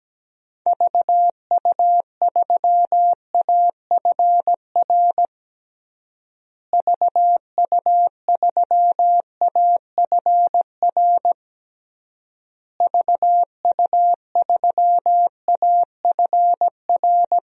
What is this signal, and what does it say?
Morse code, heard as a keyed tone, about 17 wpm, 700 Hz. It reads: VU3AFR VU3AFR VU3AFR
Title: DXCC-100 Call Signs - 6 Characters - Repeated 3x - 17wpm